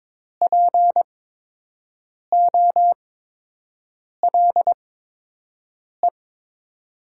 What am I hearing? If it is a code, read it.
Morse code, heard as a keyed tone, about 22 wpm, spaced as for 6 wpm, 700 Hz. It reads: POLE
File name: Top 90 QSO Elements - 8x Character Spacing - 22wpm